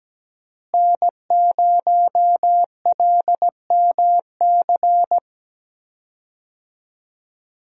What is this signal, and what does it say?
Morse code, heard as a keyed tone, about 17 wpm, 700 Hz. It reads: N0LMC
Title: USA Call Signs - 5 Characters - Repeated 3x - 17wpm